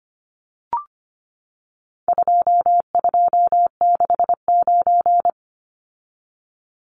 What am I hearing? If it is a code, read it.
Morse code, heard as a keyed tone, about 25 wpm, 700 Hz. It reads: 2269
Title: Numbers - 4-Digit 25wpm